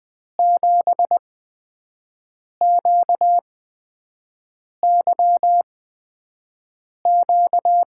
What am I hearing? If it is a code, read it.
Morse code, heard as a keyed tone, about 20 wpm, spaced as for 5 wpm, 700 Hz. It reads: 7QYQ
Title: USA Call Signs - 6 Characters - 8x Character Spacing - 20wpm